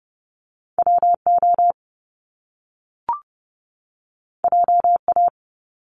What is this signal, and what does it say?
Morse code, heard as a keyed tone, about 30 wpm, 700 Hz. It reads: WO JA
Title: Top Two Letter Combinations in Words 30wpm